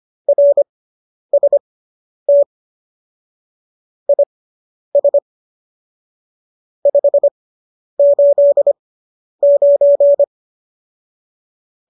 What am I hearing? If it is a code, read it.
Morse code, heard as a keyed tone, about 25 wpm, spaced as for 10 wpm, 570 Hz. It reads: RST IS 589
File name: qso005 10